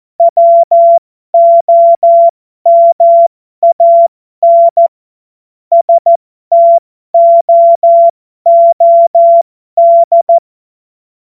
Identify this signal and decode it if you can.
Morse code, heard as a keyed tone, about 14 wpm, spaced as for 12 wpm, 680 Hz. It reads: WOMAN STOOD